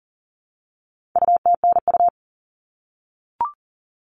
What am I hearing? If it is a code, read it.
Morse code, heard as a keyed tone, about 40 wpm, 700 Hz. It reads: UTNU
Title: All Cut Numbers - 4-Digit 40wpm